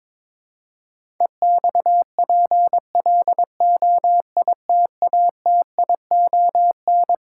Morse code, heard as a keyed tone, about 21 wpm, 700 Hz. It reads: EXPLOITATION
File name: Twelve Letter Words - 22wpm